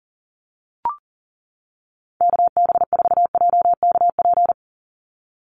Morse code, heard as a keyed tone, about 40 wpm, 700 Hz. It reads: KB4JKP
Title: USA Call Signs 40wpm